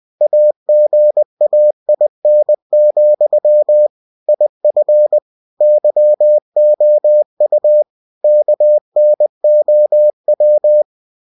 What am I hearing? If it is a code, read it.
Morse code, heard as a keyed tone, about 20 wpm, 600 Hz. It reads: AGAIN, IF YOU KNOW
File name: cw-books / winnie-the-pooh / winnie-20wpm-page-0088